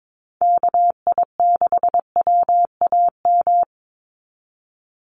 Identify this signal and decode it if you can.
Morse code, heard as a keyed tone, about 22 wpm, 700 Hz. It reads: KI6WAM
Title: USA Call Signs 22wpm